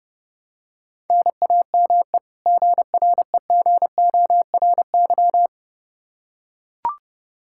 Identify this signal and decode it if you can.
Morse code, heard as a keyed tone, about 30 wpm, 700 Hz. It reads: NAME GREGORY
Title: Practice QSO - Exchange 1 - Beginning - 30wpm